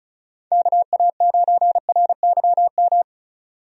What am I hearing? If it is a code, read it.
Morse code, heard as a keyed tone, about 34 wpm, 700 Hz. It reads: KA9RYM
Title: USA Call Signs - 6 Characters - Repeated 3x - 35wpm